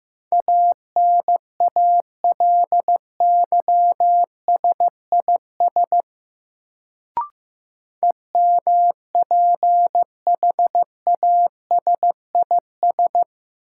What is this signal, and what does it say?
Morse code, heard as a keyed tone, about 15 wpm, 700 Hz. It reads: ANALYSIS EMPHASIS
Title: Sentences from Top 1400 Words - New Word Drill Alt - 15wpm